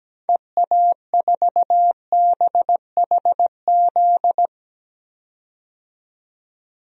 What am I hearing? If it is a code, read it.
Morse code, heard as a keyed tone, about 17 wpm, 700 Hz. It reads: EA4BHZ
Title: DXCC-100 Call Signs - 6 Characters - 17wpm